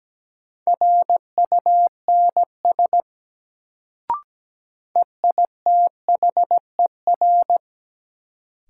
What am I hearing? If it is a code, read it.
Morse code, heard as a keyed tone, about 17 wpm, 700 Hz. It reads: RUNS EITHER
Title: Sentences from Top 600 Words - New Word Drill - 17wpm